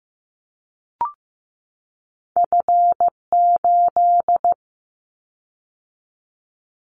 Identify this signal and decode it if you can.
Morse code, heard as a keyed tone, about 15 wpm, 700 Hz. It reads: F8
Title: Callsign Prefixes for Top 25 DXCC - 15wpm